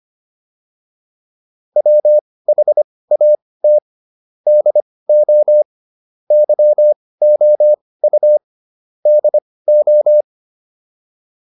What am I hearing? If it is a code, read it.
Morse code, heard as a keyed tone, about 25 wpm, spaced as for 18 wpm, 600 Hz. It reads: WHAT DO YOU DO